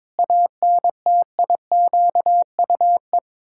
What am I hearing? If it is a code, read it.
Morse code, heard as a keyed tone, about 22 wpm, 700 Hz. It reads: ANTIQUE
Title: Pangram Challenge v2 - 22wpm